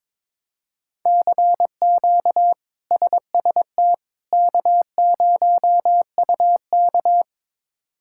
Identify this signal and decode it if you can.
Morse code, heard as a keyed tone, about 22 wpm, 700 Hz. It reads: CQ SST K0UK
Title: K1USN SST QSO Practice - 22wpm